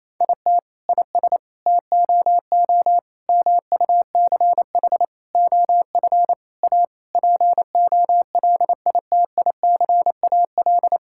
Morse code, heard as a keyed tone, about 28 wpm, 700 Hz. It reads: IT IS TOO MUCH OF A POLITICAL
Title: Sentences from Top 1000 Words - Review Part 5 - 28wpm